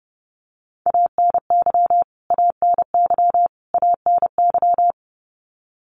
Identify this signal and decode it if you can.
Morse code, heard as a keyed tone, about 28 wpm, 700 Hz. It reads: ANY ANY ANY
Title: Top 300 Words Repeated 30wpm